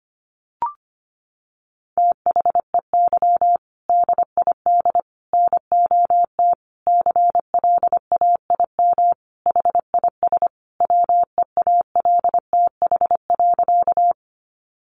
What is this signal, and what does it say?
Morse code, heard as a keyed tone, about 25 wpm, 700 Hz. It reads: THEY DID NOT CLAIM HIS WEALTH.
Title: Sentences from Top 1500 Words 25wpm